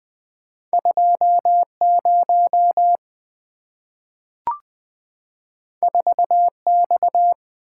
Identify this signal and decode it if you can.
Morse code, heard as a keyed tone, about 20 wpm, 690 Hz. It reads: 20 4X
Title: Two Letter-Number Combinations - 20wpm